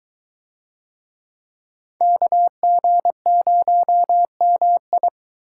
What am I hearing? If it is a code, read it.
Morse code, heard as a keyed tone, about 20 wpm, 700 Hz. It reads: KG0MI